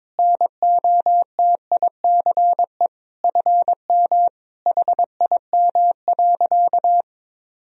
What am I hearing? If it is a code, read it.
Morse code, heard as a keyed tone, about 22 wpm, 700 Hz. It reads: NOTICE FM HIM.
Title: Sentences from Top 400 Words and Abbreviations 22wpm